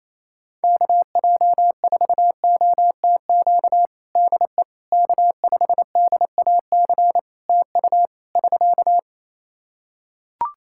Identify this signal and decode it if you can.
Morse code, heard as a keyed tone, about 28 wpm, 700 Hz. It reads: KJ4OTQ DE K5DAC TU <SK>
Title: Practice QSO - Exchange 4 - Ending - 28wpm